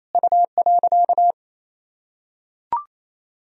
Morse code, heard as a keyed tone, about 28 wpm, 700 Hz. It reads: U.
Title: Sentences from Top 200 Words and Abbreviations 28wpm